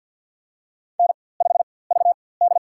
Morse code, heard as a keyed tone, about 49 wpm, spaced as for 23 wpm, 700 Hz. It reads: N5VD
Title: USA Call Signs - 4x Character Spacing - 50wpm